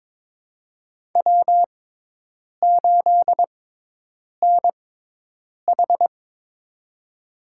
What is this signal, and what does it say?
Morse code, heard as a keyed tone, about 22 wpm, spaced as for 8 wpm, 700 Hz. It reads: W8NH
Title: USA Call Signs - 4 Characters - 6x Character Spacing - 22wpm